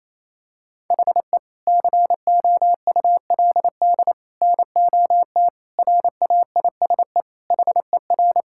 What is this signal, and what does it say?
Morse code, heard as a keyed tone, about 28 wpm, 700 Hz. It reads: HE COULD NOT RAISE HER
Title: Sentences from Top 1600 Words 28wpm